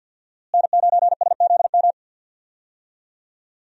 Morse code, heard as a keyed tone, about 50 wpm, 700 Hz. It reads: N9SZM
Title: USA Call Signs - 5 Characters - Repeated 3x - 50wpm